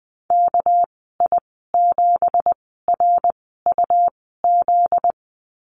Morse code, heard as a keyed tone, about 20 wpm, spaced as for 15 wpm, 700 Hz. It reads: KI7RUZ
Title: 245 Lesson 39g - Letter z - Call Signs 20wpm